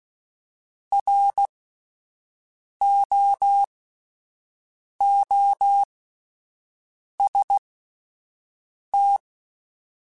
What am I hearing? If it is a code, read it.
Morse code, heard as a keyed tone, about 16 wpm, spaced as for 5 wpm, 800 Hz. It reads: ROOST